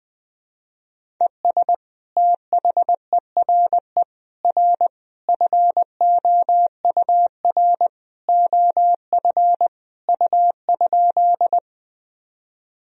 Morse code, heard as a keyed tone, about 20 wpm, 700 Hz. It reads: ES THERE R FOUR OF U?
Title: Sentences from Top 300 Words and Abbreviations 20wpm